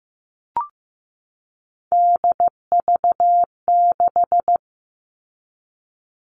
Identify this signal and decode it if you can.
Morse code, heard as a keyed tone, about 15 wpm, 700 Hz. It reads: DV6
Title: All Cut Numbers - 3-Digit 15wpm